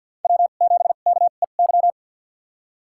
Morse code, heard as a keyed tone, about 47 wpm, spaced as for 35 wpm, 700 Hz. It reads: W7KEY
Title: USA Call Signs - 5 Characters - 2x Character Spacing - 50wpm